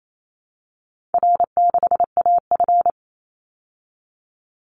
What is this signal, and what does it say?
Morse code, heard as a keyed tone, about 28 wpm, 700 Hz. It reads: R6AF
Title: Call Signs Repeated 2x - 28wpm